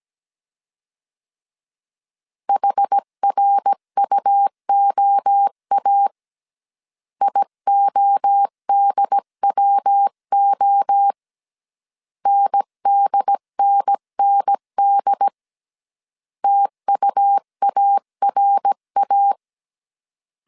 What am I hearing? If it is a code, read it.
Morse code, heard as a keyed tone, about 17 wpm, 800 Hz. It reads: HRUOA IODWO NDNND TUARA
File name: CW / DET / 1 3 1500C S A 16wpm 2141